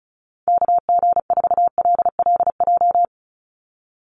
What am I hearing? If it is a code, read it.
Morse code, heard as a keyed tone, about 34 wpm, 700 Hz. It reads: KG4LLJ
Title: USA Call Signs 35wpm